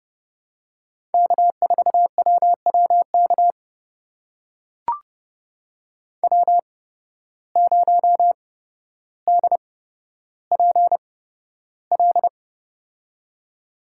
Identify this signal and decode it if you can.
Morse code, heard as a keyed tone, about 31 wpm, spaced as for 13 wpm, 700 Hz. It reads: K4WWK W0DPL
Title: USA Call Signs - 5 Characters - 8x Character Spacing - 30wpm